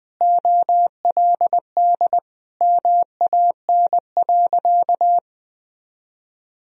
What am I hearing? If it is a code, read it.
Morse code, heard as a keyed tone, about 20 wpm, 700 Hz. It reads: OLD MAN.
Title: Sentences from Top 500 Words and Abbreviations 20wpm